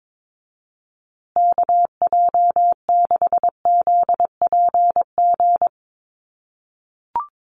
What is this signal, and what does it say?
Morse code, heard as a keyed tone, about 22 wpm, 700 Hz. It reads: KJ6ZPG
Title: USA Call Signs Repeated 3x 22wpm